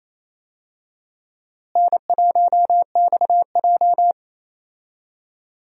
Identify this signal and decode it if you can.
Morse code, heard as a keyed tone, about 28 wpm, 700 Hz. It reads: N1XJ
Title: USA Call Signs - 4 Characters - Repeated 2x - 28wpm